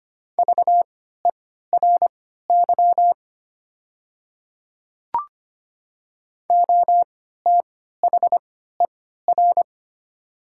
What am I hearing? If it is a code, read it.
Morse code, heard as a keyed tone, about 25 wpm, spaced as for 14 wpm, 700 Hz. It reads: VERY OTHER
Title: 3x Character Spacing - Top 200 Words - 25wpm